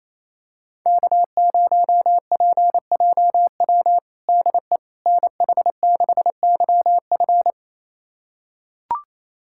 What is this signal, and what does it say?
Morse code, heard as a keyed tone, about 28 wpm, 700 Hz. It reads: K0PJW DE NH6YF
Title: USA Callsign DE Callsign - 5 Characters - 28wpm